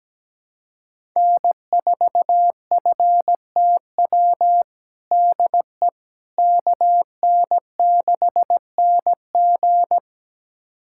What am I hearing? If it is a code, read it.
Morse code, heard as a keyed tone, about 17 wpm, 700 Hz. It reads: N4FTW DE KN6NG
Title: USA Callsign DE Callsign - 5 Characters - 17wpm